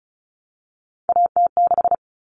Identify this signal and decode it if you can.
Morse code, heard as a keyed tone, about 34 wpm, 700 Hz. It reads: AT6
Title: All Cut Numbers - 3-Digit 35wpm